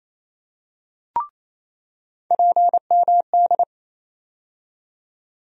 Morse code, heard as a keyed tone, about 27 wpm, 700 Hz. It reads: PMD
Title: USA Call Sign Postfixes - 3 Letters 28wpm